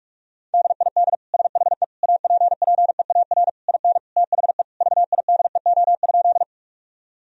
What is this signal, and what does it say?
Morse code, heard as a keyed tone, about 46 wpm, 700 Hz. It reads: DID SHE APPEAR IN THE VIDEO?
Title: Sentences from Top 1000 Words - Review Part 4 - 45wpm